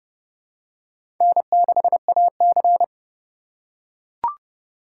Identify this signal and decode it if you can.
Morse code, heard as a keyed tone, about 30 wpm, 700 Hz. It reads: N6AC
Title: USA Call Signs - 4 Characters - 4x Character Spacing - 30wpm